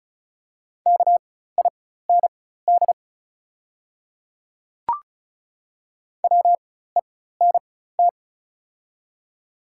Morse code, heard as a keyed tone, about 36 wpm, spaced as for 16 wpm, 700 Hz. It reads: KIND WENT
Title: 4x Character Spacing - Top 200 Words - 35wpm